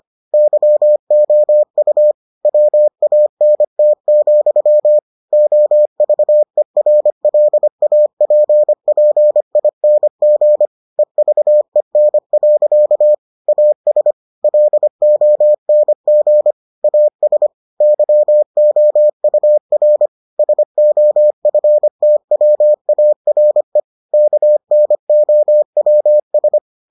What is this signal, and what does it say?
Morse code, heard as a keyed tone, about 25 wpm, 600 Hz. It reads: YOU WANT, OVERLAPPING EVEN. AS LONG AS YOUR SOFTWARE KNOWS